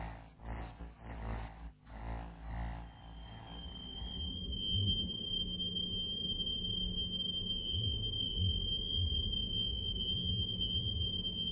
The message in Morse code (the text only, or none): none